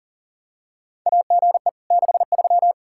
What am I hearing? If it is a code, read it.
Morse code, heard as a keyed tone, about 40 wpm, 700 Hz. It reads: AGE 63
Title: Practice QSO - Exchange 3 - More Info - 40wpm